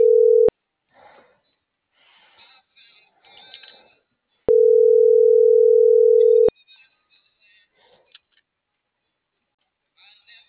Ringback tone (North America), 440 + 480 Hz.